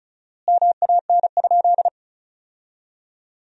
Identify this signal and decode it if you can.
Morse code, heard as a keyed tone, about 34 wpm, 700 Hz. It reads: MAN?
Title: 4x Word-Spacing - Sentences from Top 200 Words - 35wpm